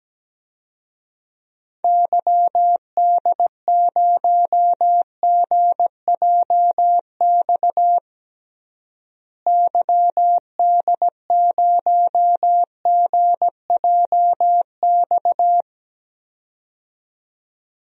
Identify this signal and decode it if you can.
Morse code, heard as a keyed tone, about 17 wpm, 700 Hz. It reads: YD0GJX YD0GJX
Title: DXCC-100 Call Signs - 6 Characters - Repeated 2x - 17wpm